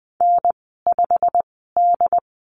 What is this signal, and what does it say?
Morse code, heard as a keyed tone, about 19 wpm, spaced as for 14 wpm, 700 Hz. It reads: N5D
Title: 140 Lesson 23g - Letter f - Call Signs 20wpm